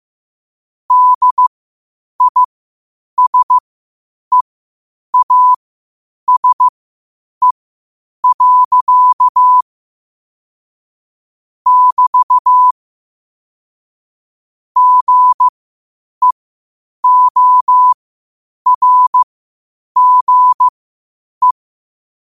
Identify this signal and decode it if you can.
Morse code, heard as a keyed tone, about 15 wpm, spaced as for 8 wpm, 1000 Hz. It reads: DISEASE. <BT> GEORGE
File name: qotdfiles / 8wpm